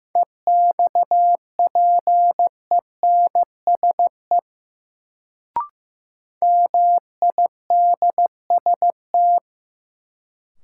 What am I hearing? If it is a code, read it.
Morse code, heard as a keyed tone, about 15 wpm, 700 Hz. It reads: EXPENSE MIDST
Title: Sets of 4 Words - New Word Drill - 15wpm